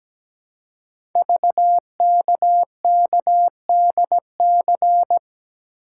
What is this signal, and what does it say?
Morse code, heard as a keyed tone, about 17 wpm, 700 Hz. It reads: VKKDC
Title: Code Groups - 5 Letters - 17wpm